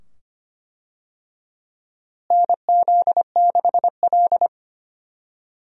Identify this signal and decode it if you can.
Morse code, heard as a keyed tone, about 25 wpm, 700 Hz. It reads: NZ6L
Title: USA Call Signs - 4 Characters - Repeated 2x - 25wpm